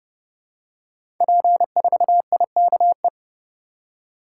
Morse code, heard as a keyed tone, about 30 wpm, 700 Hz. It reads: P4IKE
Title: Code Groups - 5 Letter:Numbers - 30wpm